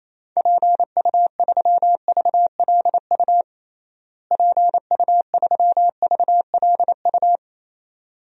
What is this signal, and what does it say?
Morse code, heard as a keyed tone, about 28 wpm, 700 Hz. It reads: PU3VLU PU3VLU
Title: DXCC-100 Call Signs - 6 Characters - Repeated 2x - 28wpm